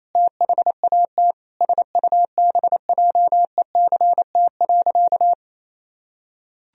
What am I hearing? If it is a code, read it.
Morse code, heard as a keyed tone, about 28 wpm, 700 Hz. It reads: THAT SUBJECT.
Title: Sentences from Top 1000 Words - Review Part 1 - 28wpm